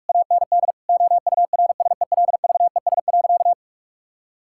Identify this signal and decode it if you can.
Morse code, heard as a keyed tone, about 45 wpm, 700 Hz. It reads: AND OURSELVES.
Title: Sentences from Top 1600 Words 45wpm